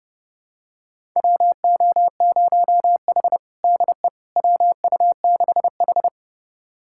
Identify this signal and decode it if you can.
Morse code, heard as a keyed tone, about 30 wpm, 700 Hz. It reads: WO0H DE WU6H
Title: USA Callsign DE Callsign - 4 Characters - 30wpm